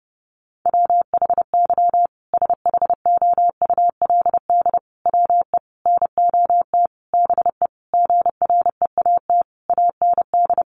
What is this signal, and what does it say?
Morse code, heard as a keyed tone, about 30 wpm, 700 Hz. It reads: WHY SHOULD WE NOT BE GREAT AND
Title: Sentences from Top 1500 Words 30wpm